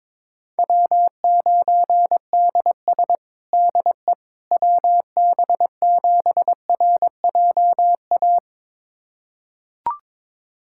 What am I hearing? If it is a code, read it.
Morse code, heard as a keyed tone, about 22 wpm, 700 Hz. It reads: W9DS DE WB7RJA
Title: USA Callsign DE Callsign 22wpm